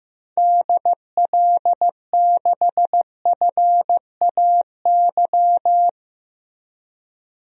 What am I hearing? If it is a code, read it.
Morse code, heard as a keyed tone, about 15 wpm, 700 Hz. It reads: DL6FAY
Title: DXCC-100 Call Signs - 6 Characters - Repeated 3x - 15wpm